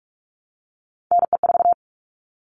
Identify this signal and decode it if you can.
Morse code, heard as a keyed tone, about 44 wpm, 700 Hz. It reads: NE4
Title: All Cut Numbers - 3-Digit 45wpm